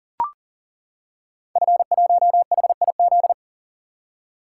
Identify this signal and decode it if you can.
Morse code, heard as a keyed tone, about 41 wpm, 700 Hz. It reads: F1HIZ
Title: Code Groups - 5 Letter:Numbers - 40wpm